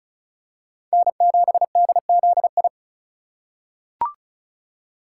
Morse code, heard as a keyed tone, about 36 wpm, 700 Hz. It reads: N7DZI
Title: USA Call Signs - 5 Characters - 3x Character Spacing - 35wpm